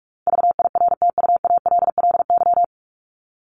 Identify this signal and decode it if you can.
Morse code, heard as a keyed tone, about 44 wpm, 700 Hz. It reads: VIRTUALLY
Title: Top 1501-1750 Words 45wpm